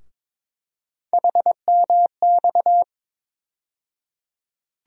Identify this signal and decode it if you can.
Morse code, heard as a keyed tone, about 22 wpm, 700 Hz. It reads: HMX
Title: USA Call Sign Postfixes - 3 Letters 22wpm